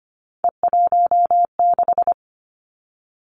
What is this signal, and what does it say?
Morse code, heard as a keyed tone, about 25 wpm, 700 Hz. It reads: E16
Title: Three Challenging Letter-Number Combinations 25wpm